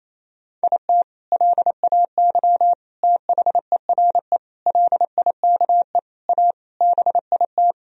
Morse code, heard as a keyed tone, about 28 wpm, 700 Hz. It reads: IT LAY THERE LIKE A BIT
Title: Sentences from Top 1000 Words - Review Part 3 - 28wpm